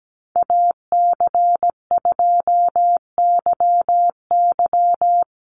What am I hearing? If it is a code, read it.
Morse code, heard as a keyed tone, about 17 wpm, 700 Hz. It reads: AC2YY
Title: USA Call Signs 17wpm